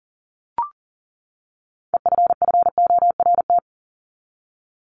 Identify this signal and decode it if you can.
Morse code, heard as a keyed tone, about 40 wpm, 700 Hz. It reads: EFFORT
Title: Top 1001-1250 Words 40wpm